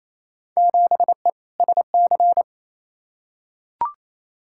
Morse code, heard as a keyed tone, about 28 wpm, 700 Hz. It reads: 7E SC
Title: Field Day Exchanges Encore - 28wpm